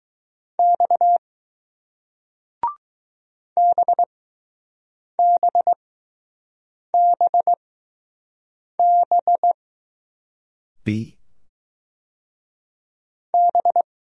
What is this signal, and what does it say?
Morse code, heard as a keyed tone, about 20 wpm, spaced as for 6 wpm, 700 Hz. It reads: X BBBB B